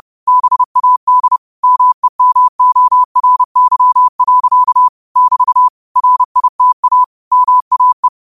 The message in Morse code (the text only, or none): BAD MEMORY. <BT> RITA MAE